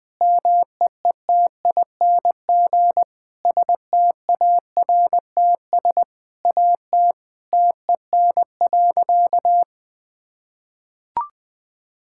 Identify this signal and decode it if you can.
Morse code, heard as a keyed tone, about 20 wpm, 700 Hz. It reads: MEETING STARTS AT TEN.
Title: Sentences with 7-Word Phrases - Spoken Beginning - 20wpm